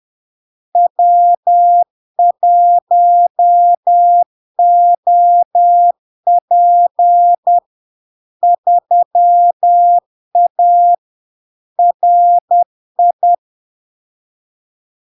Morse code, heard as a keyed tone, about 10 wpm, 700 Hz. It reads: W1OP 3A RI
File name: LARK-FD-2022-Log-in-CW-10wpm-700-Hz